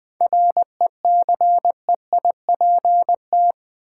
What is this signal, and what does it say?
Morse code, heard as a keyed tone, about 20 wpm, 690 Hz. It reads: RECEIPT
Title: 7 Letter Words - First 3 Letters Given - 20wpm